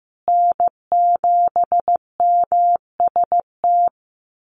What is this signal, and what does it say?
Morse code, heard as a keyed tone, about 15 wpm, 700 Hz. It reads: N7MST
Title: USA Call Signs Repeated 2x 15wpm